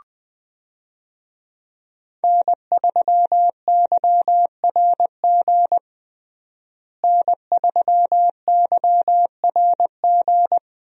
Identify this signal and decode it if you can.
Morse code, heard as a keyed tone, about 20 wpm, 700 Hz. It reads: N3YRG N3YRG
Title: USA Call Signs - 5 Characters - Repeated 3x - 20wpm